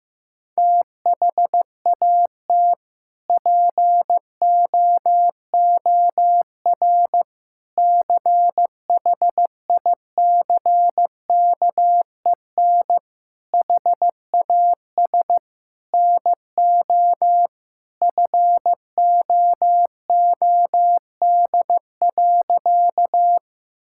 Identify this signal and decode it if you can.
Morse code, heard as a keyed tone, about 15 wpm, 700 Hz. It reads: THAT POOR CHICKEN HAS NO FOOD.
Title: Sentences from Top 1000 Words - Review Part 5 - 15wpm